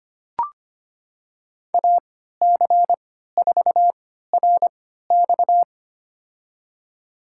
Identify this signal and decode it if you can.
Morse code, heard as a keyed tone, about 25 wpm, spaced as for 14 wpm, 700 Hz. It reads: AC4RX
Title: USA Call Signs - 3x Character Spacing - 25wpm